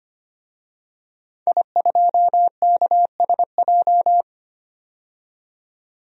Morse code, heard as a keyed tone, about 25 wpm, 700 Hz. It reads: I2KSJ
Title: DXCC-100 Call Signs - 5 Characters - 25wpm